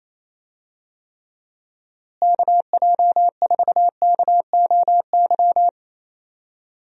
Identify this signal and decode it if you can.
Morse code, heard as a keyed tone, about 27 wpm, 700 Hz. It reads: KJ4KOY
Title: USA Call Signs - 6 Characters - Repeated 3x - 28wpm